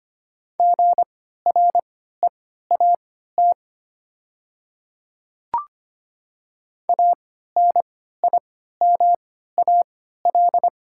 Morse code, heard as a keyed tone, about 25 wpm, spaced as for 14 wpm, 700 Hz. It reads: GREAT ANIMAL
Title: 3x Character Spacing - Top 300 Words - 25wpm